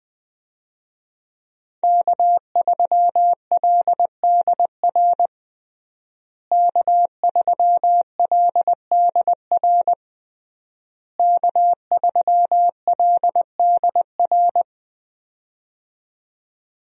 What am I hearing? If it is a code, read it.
Morse code, heard as a keyed tone, about 20 wpm, 700 Hz. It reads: K3LDR K3LDR K3LDR
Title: USA Call Signs - 5 Characters - Repeated 3x - 20wpm